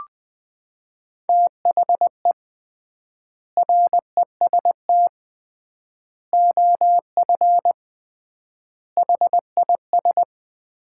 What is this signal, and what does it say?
Morse code, heard as a keyed tone, about 20 wpm, 700 Hz. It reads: THE REST OF HIS
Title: 3x Word-Spacing - Sets of 5 Words - 20wpm